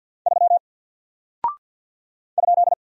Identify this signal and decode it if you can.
Morse code, heard as a keyed tone, about 49 wpm, 700 Hz. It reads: 3 ?
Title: Single Character (Letter:Number:Punctuation:Prosign) Alt 50wpm